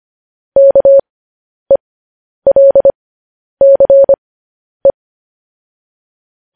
Morse code, heard as a keyed tone, about 25 wpm, spaced as for 10 wpm, 550 Hz. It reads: KELCE
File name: podcast / News headlines-1x-25x10